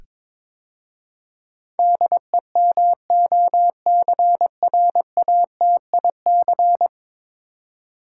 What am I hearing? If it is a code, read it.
Morse code, heard as a keyed tone, about 21 wpm, 700 Hz. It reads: DEMOCRATIC
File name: practice / 22wpm / 10 Letter Words - First 3 Letters Given - 22wpm